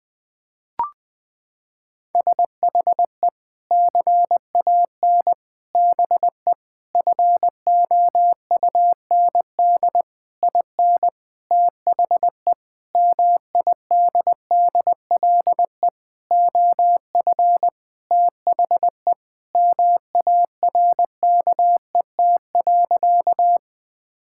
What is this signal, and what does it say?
Morse code, heard as a keyed tone, about 20 wpm, 700 Hz. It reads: SHE CAN BE FOUND IN THE MIDDLE OF THE MARKET.
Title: Sentences from Top 1000 Words - Review Part 6 - 20wpm